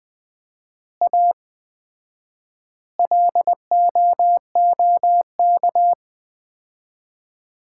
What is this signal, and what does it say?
Morse code, heard as a keyed tone, about 20 wpm, 700 Hz. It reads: A LOOK
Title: 4x Word-Spacing - Sets of 5 Words - 20wpm